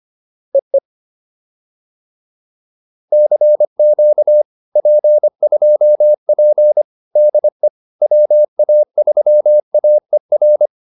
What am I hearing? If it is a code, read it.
Morse code, heard as a keyed tone, about 25 wpm, 600 Hz. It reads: EE CQ P2P DE WA3AER
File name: POTA306a 25